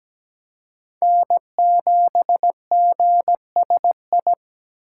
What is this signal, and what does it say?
Morse code, heard as a keyed tone, about 17 wpm, 700 Hz. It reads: N7GSI
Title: USA Call Signs - 5 Characters - 2x Character Spacing - 17wpm